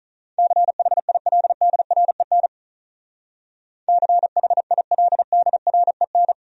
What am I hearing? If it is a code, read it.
Morse code, heard as a keyed tone, about 43 wpm, 700 Hz. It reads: CHILDREN CHILDREN
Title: Speed-Racing Top 500 Words 35wpm